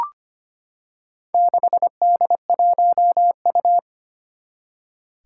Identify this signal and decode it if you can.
Morse code, heard as a keyed tone, about 25 wpm, 700 Hz. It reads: 6D1U
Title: Four Challenging Letter-Number Combinations 25wpm